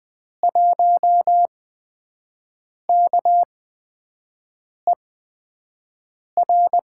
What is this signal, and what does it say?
Morse code, heard as a keyed tone, about 20 wpm, spaced as for 5 wpm, 700 Hz. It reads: 1KER
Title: USA Call Signs - 5 Characters - 8x Character Spacing - 20wpm